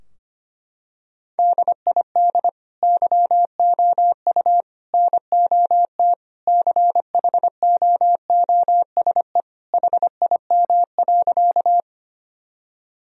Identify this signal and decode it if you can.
Morse code, heard as a keyed tone, about 25 wpm, 700 Hz. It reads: DID YOU NOT CHOOSE HIM.
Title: Sentences from Top 1000 Words - Review Part 4 - 25wpm